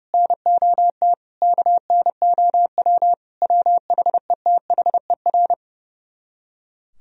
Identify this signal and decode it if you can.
Morse code, heard as a keyed tone, about 37 wpm, 700 Hz. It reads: NOT KNOW WHETHER
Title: Speed-Racing - Sets of 5 Words 30wpm